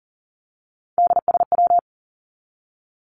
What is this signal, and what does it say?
Morse code, heard as a keyed tone, about 40 wpm, 700 Hz. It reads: DSW
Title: Abbreviations 40wpm